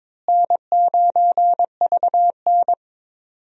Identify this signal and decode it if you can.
Morse code, heard as a keyed tone, about 22 wpm, 700 Hz. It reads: N9VN
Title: USA Call Signs - 4 Characters - Repeated 3x - 22wpm